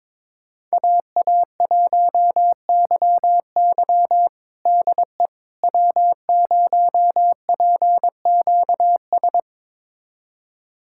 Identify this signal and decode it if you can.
Morse code, heard as a keyed tone, about 22 wpm, 700 Hz. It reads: AA1YY DE W0PQS
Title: USA Callsign DE Callsign - 5 Characters - 22wpm